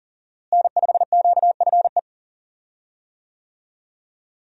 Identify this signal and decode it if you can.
Morse code, heard as a keyed tone, about 39 wpm, 700 Hz. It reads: N5QFE
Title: USA Call Signs - 5 Characters - Repeated 3x - 40wpm